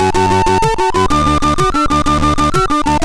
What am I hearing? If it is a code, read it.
There is no code.